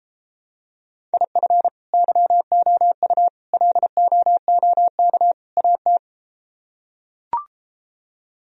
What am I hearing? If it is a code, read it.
Morse code, heard as a keyed tone, about 33 wpm, 700 Hz. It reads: IF YOU LOOK AT